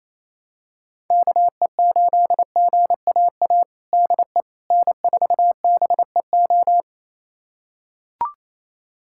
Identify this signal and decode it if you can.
Morse code, heard as a keyed tone, about 28 wpm, 700 Hz. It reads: KE8GAA DE N4BEO
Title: USA Callsign DE Callsign 28wpm